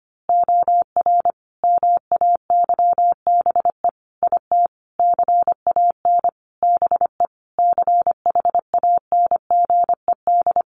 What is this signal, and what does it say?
Morse code, heard as a keyed tone, about 25 wpm, 700 Hz. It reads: OR MAYBE IT CAN BE CHANGED
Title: A Sound Of Thunder 25wpm